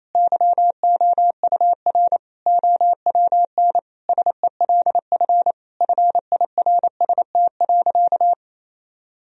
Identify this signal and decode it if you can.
Morse code, heard as a keyed tone, about 28 wpm, 700 Hz. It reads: YOUR OWN SELF FIRST.
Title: Sentences from Top 1000 Words - Review Part 1 - 28wpm